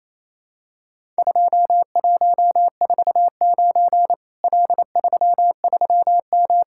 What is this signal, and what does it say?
Morse code, heard as a keyed tone, about 27 wpm, 700 Hz. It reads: 2149 L33M
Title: Worked All Europe DX Contest - QTC Exchange - 28wpm